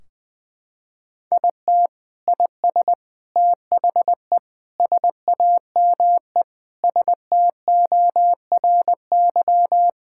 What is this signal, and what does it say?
Morse code, heard as a keyed tone, about 21 wpm, 700 Hz. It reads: IT IS THE SAME STORY
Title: Sentences from Top 1600 Words 20wpm